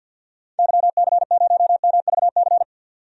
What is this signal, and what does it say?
Morse code, heard as a keyed tone, about 50 wpm, 700 Hz. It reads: YC0MVC
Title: DXCC-100 Call Signs - 6 Characters - 50wpm